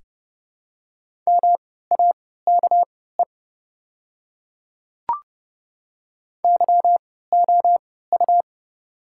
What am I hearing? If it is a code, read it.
Morse code, heard as a keyed tone, about 30 wpm, spaced as for 17 wpm, 700 Hz. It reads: MAKE YOU
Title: 3x Character Spacing - Top 100 Words - 30wpm